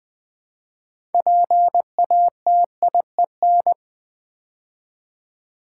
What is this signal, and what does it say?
Morse code, heard as a keyed tone, about 20 wpm, 700 Hz. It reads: PATIEN
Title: Build A Word - 2 Letters At A Time - 8-Letter Words - 20wpm